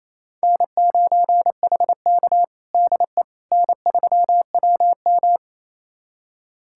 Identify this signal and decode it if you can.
Morse code, heard as a keyed tone, about 28 wpm, 700 Hz. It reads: N9HK DE N3WM
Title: USA Callsign DE Callsign - 4 Characters - 28wpm